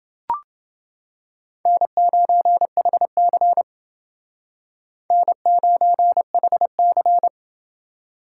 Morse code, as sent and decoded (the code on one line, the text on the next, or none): -. ----. .... -.-. / -. ----. .... -.-.
N9HC N9HC